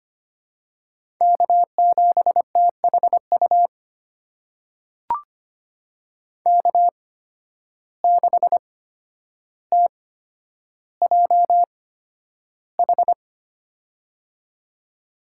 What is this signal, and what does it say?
Morse code, heard as a keyed tone, about 25 wpm, spaced as for 11 wpm, 700 Hz. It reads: K7THU K6TJH
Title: USA Call Signs - 5 Characters - 8x Character Spacing - 25wpm